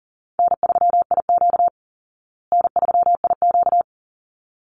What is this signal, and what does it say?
Morse code, heard as a keyed tone, about 37 wpm, 700 Hz. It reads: N3IQ N3IQ
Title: Easy Call Signs Repeated 2x - 40wpm